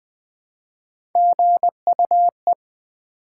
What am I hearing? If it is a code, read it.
Morse code, heard as a keyed tone, about 20 wpm, 700 Hz. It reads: GUE
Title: Build A Word - 1 Letter At A Time - 5-Letter Words - 20wpm